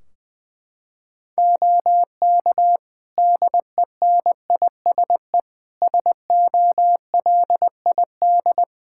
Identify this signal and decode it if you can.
Morse code, heard as a keyed tone, about 19 wpm, 700 Hz. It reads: OK DENISE SOLID
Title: K1USN SST QSO Practice 20wpm